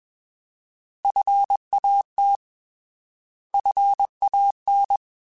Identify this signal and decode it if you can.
Morse code, heard as a keyed tone, about 21 wpm, 800 Hz. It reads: FAT FAN
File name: Alphabetical / Alphabetical mp3s / Fs @20